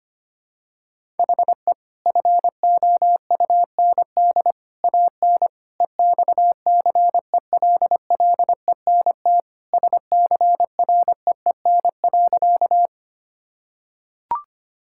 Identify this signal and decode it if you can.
Morse code, heard as a keyed tone, about 25 wpm, 700 Hz. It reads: HE FOUND AN EXCELLENT SCREEN.
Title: Sentences from Top 1000 Words - Review Part 6 - 25wpm